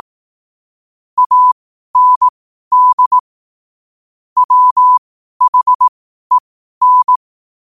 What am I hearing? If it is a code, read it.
Morse code, heard as a keyed tone, about 18 wpm, spaced as for 12 wpm, 1000 Hz. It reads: AND WHEN
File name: qotdfiles / 12wpm